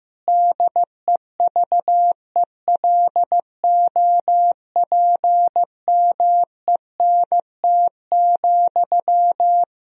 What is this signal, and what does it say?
Morse code, heard as a keyed tone, about 15 wpm, 700 Hz. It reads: DEVELOPMENT,
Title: Sentences from Top 1000 Words - Review Part 4 - 15wpm